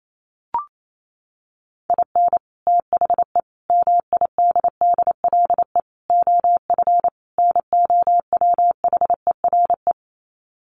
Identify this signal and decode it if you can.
Morse code, heard as a keyed tone, about 28 wpm, 700 Hz. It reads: IN THE MIDDLE OF NOWHERE
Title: Sets of 5 Words Encore 28wpm